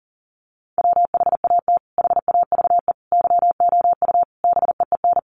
Morse code, heard as a keyed tone, about 37 wpm, 700 Hz. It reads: WHAT HAVE YOU BEEN
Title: Sentences from Top 200 Words 40wpm